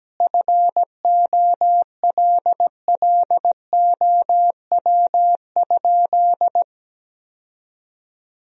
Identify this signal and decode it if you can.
Morse code, heard as a keyed tone, about 17 wpm, 700 Hz. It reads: FOLLOW?
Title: Sentences from Top 1000 Words - Review Part 3 - 17wpm